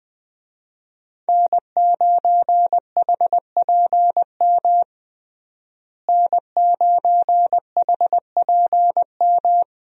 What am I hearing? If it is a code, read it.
Morse code, heard as a keyed tone, about 20 wpm, 700 Hz. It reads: N9HPM N9HPM
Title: USA Call Signs - 5 Characters - Repeated 3x - 20wpm